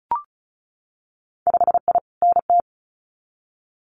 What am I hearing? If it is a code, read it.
Morse code, heard as a keyed tone, about 34 wpm, 700 Hz. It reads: 5I NT
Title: Winter Field Day Exchanges Encore 35wpm